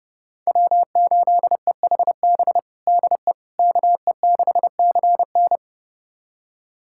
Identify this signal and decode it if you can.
Morse code, heard as a keyed tone, about 31 wpm, 700 Hz. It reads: W8EHB DE KE6CN
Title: USA Callsign DE Callsign - 5 Characters - 30wpm